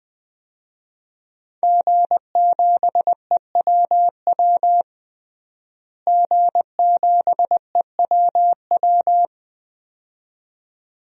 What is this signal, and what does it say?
Morse code, heard as a keyed tone, about 20 wpm, 700 Hz. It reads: G7EWW G7EWW
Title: DXCC-100 Call Signs - 5 Characters - Repeated 2x - 20wpm